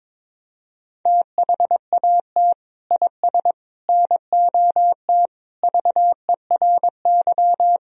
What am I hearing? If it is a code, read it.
Morse code, heard as a keyed tone, about 22 wpm, 700 Hz. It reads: THAT IS NOT VERY
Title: Sentences from Top 1000 Words - Review Part 4 - 22wpm